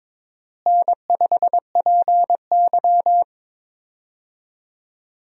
Morse code, heard as a keyed tone, about 22 wpm, 700 Hz. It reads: N5PY
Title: USA Call Signs - 4 Characters - Repeated 3x - 22wpm